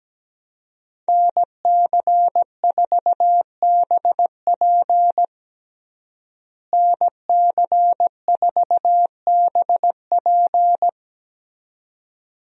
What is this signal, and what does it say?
Morse code, heard as a keyed tone, about 17 wpm, 700 Hz. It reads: NC4BP NC4BP
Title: USA Call Signs - 5 Characters - Repeated 2x - 17wpm